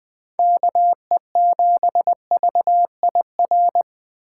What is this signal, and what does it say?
Morse code, heard as a keyed tone, about 21 wpm, 700 Hz. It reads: KE7VIR